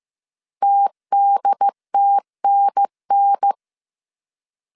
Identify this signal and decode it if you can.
Morse code, heard as a keyed tone, about 15 wpm, 800 Hz. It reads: TDTNN